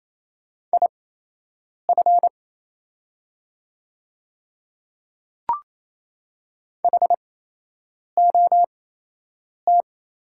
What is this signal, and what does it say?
Morse code, heard as a keyed tone, about 28 wpm, spaced as for 8 wpm, 690 Hz. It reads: IF HOT
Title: 8x Character Spacing - Top 100 Words - 28wpm